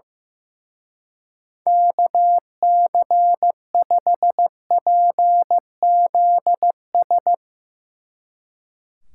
Morse code, heard as a keyed tone, about 15 wpm, 700 Hz. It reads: KC5PZS
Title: Speed Racing - USA 2x3 Call Signs 15wpm